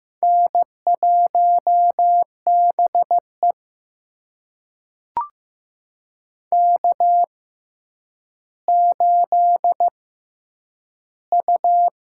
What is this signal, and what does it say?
Morse code, heard as a keyed tone, about 15 wpm, spaced as for 9 wpm, 700 Hz. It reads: N1BE K8U
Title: USA Call Signs - 4 Characters - 6x Character Spacing - 15wpm